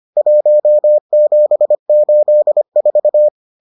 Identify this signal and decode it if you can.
Morse code, heard as a keyed tone, about 25 wpm, 600 Hz. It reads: 1784